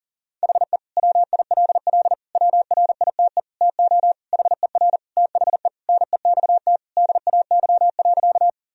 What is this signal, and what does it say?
Morse code, heard as a keyed tone, about 40 wpm, 700 Hz. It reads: HE WILL WRITE TO HER THE NEXT DAY.
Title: Sentences from Top 1000 Words - Review Part 1 - 40wpm